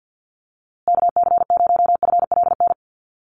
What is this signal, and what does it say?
Morse code, heard as a keyed tone, about 51 wpm, 700 Hz. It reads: KC0FLN
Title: USA Call Signs Repeated 3x 50wpm